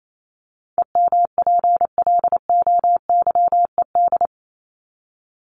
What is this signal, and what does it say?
Morse code, heard as a keyed tone, about 28 wpm, 700 Hz. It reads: EMPLOYED
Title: ED Suffix - Top 100 Words with Base Words 28wpm